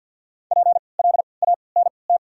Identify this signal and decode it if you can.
Morse code, heard as a keyed tone, about 54 wpm, spaced as for 29 wpm, 690 Hz. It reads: PLANT
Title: 3x Character Spacing - Top 300 Words - 50wpm